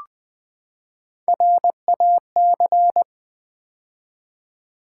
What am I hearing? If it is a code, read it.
Morse code, heard as a keyed tone, about 20 wpm, 700 Hz. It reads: RAC
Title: Build A Word - 3 Letters At A Time - 6-Letter Words - 20wpm